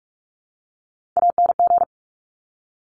Morse code, heard as a keyed tone, about 46 wpm, 700 Hz. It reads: ANG
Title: Top Three Letter Combinations in Words 45wpm